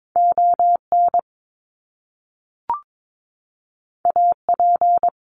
Morse code, heard as a keyed tone, about 22 wpm, 700 Hz. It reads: ON AP
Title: Top Two Letter Combinations in Words 22wpm